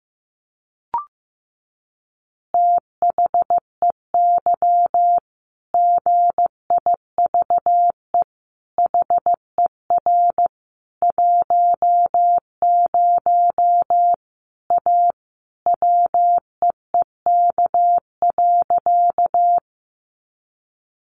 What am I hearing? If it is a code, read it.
Morse code, heard as a keyed tone, about 15 wpm, 700 Hz. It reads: THEY GIVE HER 10 A WEEK.
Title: Easy Phrases 15wpm